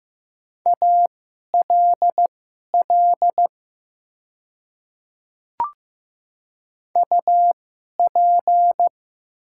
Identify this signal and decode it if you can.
Morse code, heard as a keyed tone, about 15 wpm, spaced as for 11 wpm, 700 Hz. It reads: ALL UP
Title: 2x Character Spacing - Top 100 Words - 15wpm